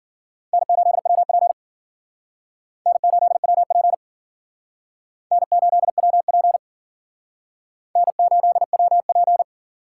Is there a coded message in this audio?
Yes: Morse code from 0.5 to 9.4 s, heard as a keyed tone, about 52 wpm, 700 Hz. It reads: N8WP N8WP N8WP N8WP